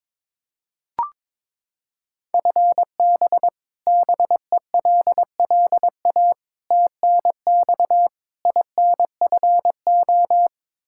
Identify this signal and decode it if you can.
Morse code, heard as a keyed tone, about 22 wpm, 700 Hz. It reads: FB BELLA TNX INFO
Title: Practice QSO - Exchange 3 - More Info - 22wpm